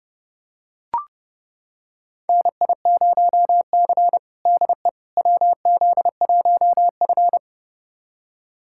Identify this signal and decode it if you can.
Morse code, heard as a keyed tone, about 30 wpm, 700 Hz. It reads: NI0C DE WZ1F
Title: USA Callsign DE Callsign - 4 Characters - 30wpm